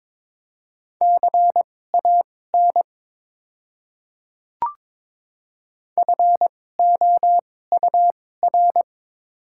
Morse code, heard as a keyed tone, about 22 wpm, spaced as for 16 wpm, 700 Hz. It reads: CAN FOUR
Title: 2x Character Spacing - Top 300 Words - 22wpm